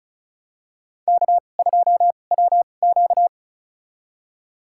Morse code, heard as a keyed tone, about 34 wpm, spaced as for 25 wpm, 700 Hz. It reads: K2WQ
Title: USA Call Signs - 2x Character Spacing - 35wpm